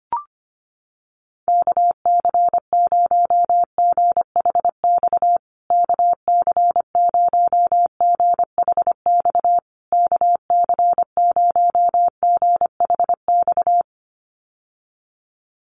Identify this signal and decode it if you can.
Morse code, heard as a keyed tone, about 25 wpm, 700 Hz. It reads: KC0GHX KC0GHX KC0GHX
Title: USA Call Signs Repeated 3x 25wpm